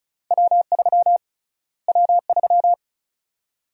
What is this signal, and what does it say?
Morse code, heard as a keyed tone, about 36 wpm, 700 Hz. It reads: W3 W3
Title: USA Call Sign Prefixes - Repeated 2x - 35wpm